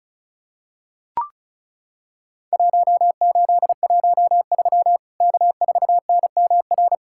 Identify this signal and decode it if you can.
Morse code, heard as a keyed tone, about 36 wpm, 700 Hz. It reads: 1813 K4NMR
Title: Worked All Europe DX Contest - QTC Exchange - 35wpm